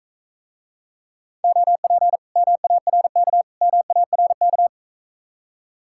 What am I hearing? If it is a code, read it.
Morse code, heard as a keyed tone, about 42 wpm, 700 Hz. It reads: OP MARK MARK